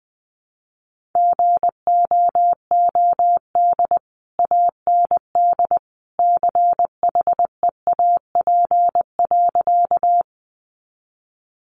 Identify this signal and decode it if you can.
Morse code, heard as a keyed tone, about 20 wpm, 700 Hz. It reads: GOOD AND CHEAP.
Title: Sentences from Top 1200 Words 20wpm